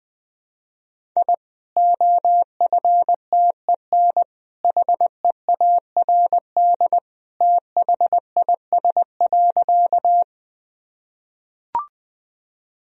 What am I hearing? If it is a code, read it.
Morse code, heard as a keyed tone, about 20 wpm, 700 Hz. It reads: I OFTEN HEARD THIS.
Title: Sentences from Top 1000 Words - Review Part 1 - 20wpm